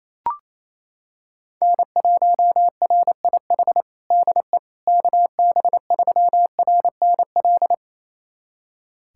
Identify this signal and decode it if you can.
Morse code, heard as a keyed tone, about 28 wpm, 700 Hz. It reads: N1RIH DE KB3RNL
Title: USA Callsign DE Callsign 28wpm